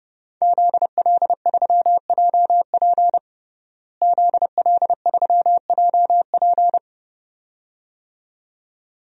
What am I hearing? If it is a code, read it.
Morse code, heard as a keyed tone, about 30 wpm, 700 Hz. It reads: ZL3JP ZL3JP
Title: DXCC-100 Call Signs - 5 Characters - Repeated 2x - 30wpm